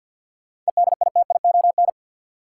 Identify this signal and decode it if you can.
Morse code, heard as a keyed tone, about 46 wpm, 700 Hz. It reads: EDITION
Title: Sets of 5 Words - New Word Drill - 50wpm